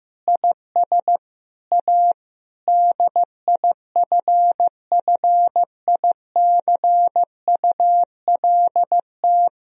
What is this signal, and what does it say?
Morse code, heard as a keyed tone, about 15 wpm, 700 Hz. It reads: IS A DIFFICULT
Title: Sentences from Top 1000 Words - Review Part 3 - 15wpm